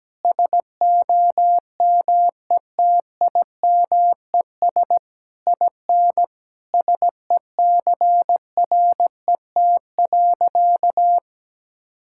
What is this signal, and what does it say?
Morse code, heard as a keyed tone, about 17 wpm, 700 Hz. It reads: SOMETIMES IN SECRET.
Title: Sentences from Top 1600 Words 17wpm